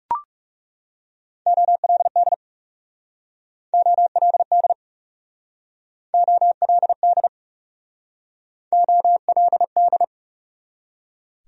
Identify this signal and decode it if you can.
Morse code, heard as a keyed tone, about 36 wpm, 700 Hz. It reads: OLD OLD OLD OLD
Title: Speed-Racing Top 300 Words 30wpm